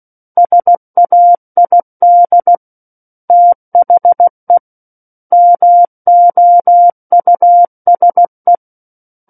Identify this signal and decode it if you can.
Morse code, heard as a keyed tone, about 16 wpm, 700 Hz. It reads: SAID THE MOUSE
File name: 03-Chapter-03 16 WPM